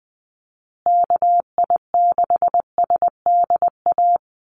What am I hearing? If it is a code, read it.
Morse code, heard as a keyed tone, about 20 wpm, 700 Hz. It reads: KI6SDA
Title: USA Call Signs Repeated 3x 20wpm